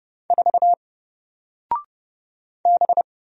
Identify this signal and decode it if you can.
Morse code, heard as a keyed tone, about 29 wpm, 700 Hz. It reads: 4 B
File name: Single-Letter-Number-30wpm